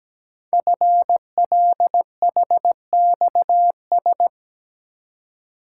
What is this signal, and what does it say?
Morse code, heard as a keyed tone, about 17 wpm, 700 Hz. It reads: FLHXS
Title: Code Groups - 5 Letters - 17wpm